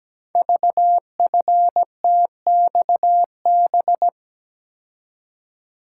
Morse code, heard as a keyed tone, about 17 wpm, 700 Hz. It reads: VFTXB
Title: Code Groups - 5 Letters - 17wpm